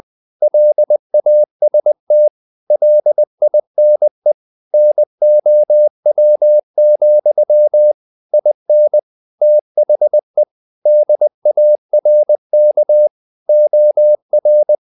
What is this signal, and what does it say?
Morse code, heard as a keyed tone, about 20 wpm, 600 Hz. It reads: LAST LINE NOW, IN THE DARK OR
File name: the old man and the sea / the old man and the sea part 1 20wpm